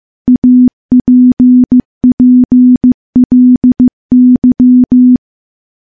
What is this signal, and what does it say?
Morse code, heard as a keyed tone, about 15 wpm, 250 Hz. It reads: APPLY
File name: morse / morse-training-15WPM